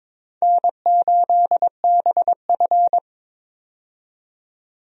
Morse code, heard as a keyed tone, about 21 wpm, 700 Hz. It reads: N8BF
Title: USA Call Signs - 4 Characters - 22wpm